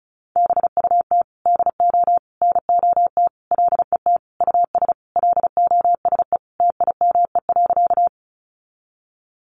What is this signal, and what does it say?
Morse code, heard as a keyed tone, about 34 wpm, 700 Hz. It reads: BUT DO NOT LET US LOSE TIME.
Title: Sentences from Top 1100 Words 35wpm